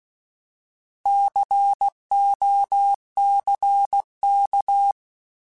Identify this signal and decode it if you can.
Morse code, heard as a keyed tone, about 16 wpm, 800 Hz. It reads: COCK
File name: Alphabetical / Alphabetical mp3s / Cs @15